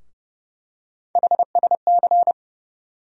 Morse code, heard as a keyed tone, about 30 wpm, 700 Hz. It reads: HSC
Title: USA Call Sign Postfixes - 3 Letters 30wpm